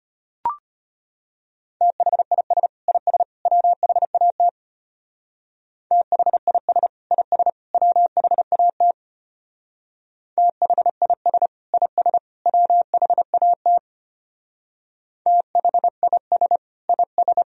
Morse code, heard as a keyed tone, about 30 wpm, 700 Hz. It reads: THIS IS WHAT THIS IS WHAT THIS IS WHAT THIS IS
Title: Speed-Racing - Sets of 3 Words 25wpm